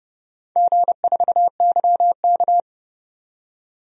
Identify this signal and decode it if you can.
Morse code, heard as a keyed tone, about 30 wpm, 700 Hz. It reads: G4YK
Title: DXCC-100 Call Signs - 4 Characters - 30wpm